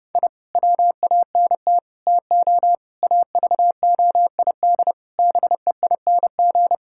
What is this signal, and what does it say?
Morse code, heard as a keyed tone, about 30 wpm, 700 Hz. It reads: I WANT TO AVOID BEING
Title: Sentences from Top 1000 Words - Review Part 6 - 30wpm